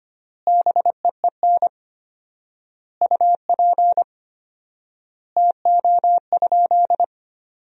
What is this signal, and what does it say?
Morse code, heard as a keyed tone, about 25 wpm, 700 Hz. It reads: BEEN UP TO?
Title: 4x Word-Spacing - Sentences from Top 100 Words - 25wpm